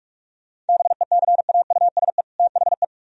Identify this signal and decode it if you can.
Morse code, heard as a keyed tone, about 56 wpm, 700 Hz. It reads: BECAUSE THE
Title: Speed-Racing - Sets of 2 Words 45wpm